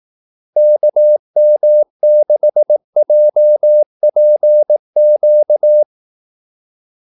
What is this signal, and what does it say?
Morse code, heard as a keyed tone, about 18 wpm, 600 Hz. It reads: KM6JPQ
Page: Kaldesignaler 18 wpm